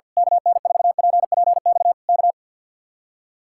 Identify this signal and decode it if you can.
Morse code, heard as a keyed tone, about 48 wpm, 700 Hz. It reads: KN4PPX K
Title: Speed-Racing Common QSO Phrases - 50wpm